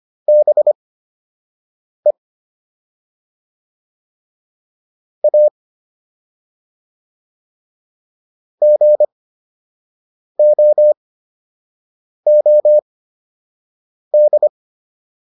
Morse code, heard as a keyed tone, about 25 wpm, spaced as for 6 wpm, 600 Hz. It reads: BE A GOOD